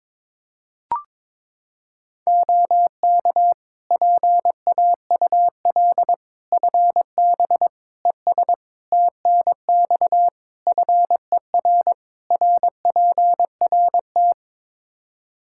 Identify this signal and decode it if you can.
Morse code, heard as a keyed tone, about 22 wpm, 700 Hz. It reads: OK PAUL FB ES TNX FER RPRT
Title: K1USN SST QSO Practice 22wpm